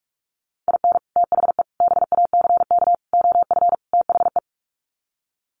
Morse code, heard as a keyed tone, about 44 wpm, 700 Hz. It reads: IN THE BACK OF THE
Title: Sets of 5 Words 45wpm